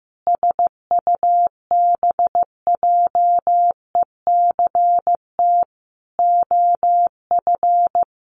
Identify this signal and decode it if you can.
Morse code, heard as a keyed tone, about 15 wpm, 700 Hz. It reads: SUBJECT OF
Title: Sentences from Top 1100 Words 15wpm